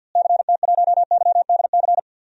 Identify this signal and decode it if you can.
Morse code, heard as a keyed tone, about 53 wpm, 700 Hz. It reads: CT1YDC
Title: DXCC-100 Call Signs - 6 Characters - Repeated 3x - 50wpm